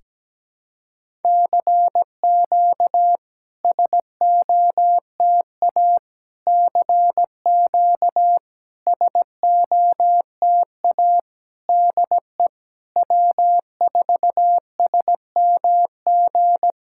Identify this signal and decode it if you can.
Morse code, heard as a keyed tone, about 17 wpm, 700 Hz. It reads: CQ SOTA CQ SOTA DE W4SMG
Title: SOTA QSO Practice 17wpm